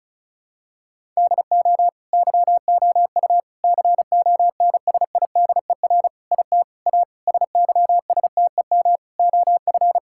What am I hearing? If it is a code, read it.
Morse code, heard as a keyed tone, about 34 wpm, 700 Hz. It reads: DO YOU CONSIDER IT A SYSTEM OF
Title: Sentences from Top 1000 Words - Review Part 6 - 35wpm